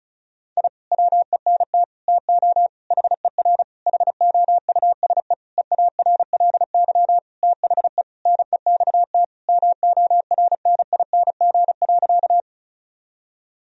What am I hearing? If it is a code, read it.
Morse code, heard as a keyed tone, about 36 wpm, 700 Hz. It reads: I WENT TO HER HOUSE EARLY THE NEXT MORNING.
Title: Sentences from Top 1000 Words - Review Part 2 - 35wpm